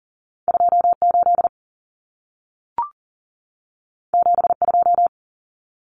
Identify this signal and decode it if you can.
Morse code, heard as a keyed tone, about 40 wpm, 700 Hz. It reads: 28 72
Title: Numbers - 2-Digit 40wpm